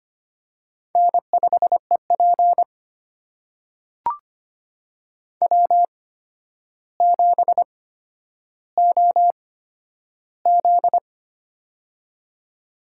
Morse code, heard as a keyed tone, about 25 wpm, spaced as for 11 wpm, 700 Hz. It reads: N5EP W7OZ
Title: USA Call Signs - 4 Characters - 8x Character Spacing - 25wpm